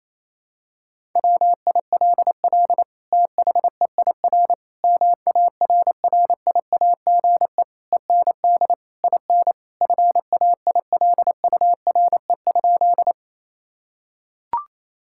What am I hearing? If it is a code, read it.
Morse code, heard as a keyed tone, about 28 wpm, 700 Hz. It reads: WILL THEIR MARRIAGE END IN FAILURE?
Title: Sentences from Top 1000 Words - Review Part 6 - 28wpm